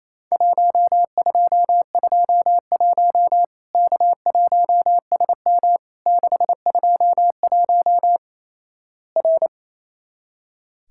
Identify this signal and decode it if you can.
Morse code, heard as a keyed tone, about 28 wpm, 700 Hz. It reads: 1221 K1SM 621 R
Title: Worked All Europe DX Contest - QTC Exchange - 28wpm